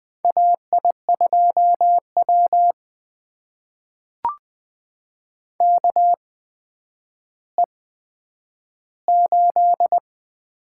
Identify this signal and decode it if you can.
Morse code, heard as a keyed tone, about 20 wpm, spaced as for 10 wpm, 700 Hz. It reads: AI2W KE8
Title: USA Call Signs - 4 Characters - 8x Character Spacing - 20wpm